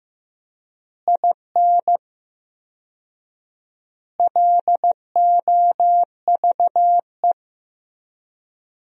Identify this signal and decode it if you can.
Morse code, heard as a keyed tone, about 15 wpm, 700 Hz. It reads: IN LOVE
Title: 4x Word-Spacing - Sentences from Top 400 Words - 15wpm